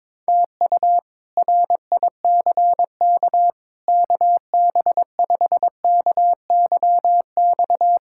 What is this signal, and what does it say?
Morse code, heard as a keyed tone, about 22 wpm, 700 Hz. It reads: TU RICK KB5KYX
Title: K1USN SST QSO Practice - 22wpm